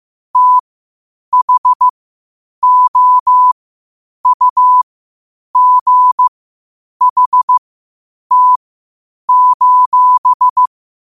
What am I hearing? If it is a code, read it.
Morse code, heard as a keyed tone, about 15 wpm, spaced as for 8 wpm, 1000 Hz. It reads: THOUGHT:
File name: qotdfiles / 8wpm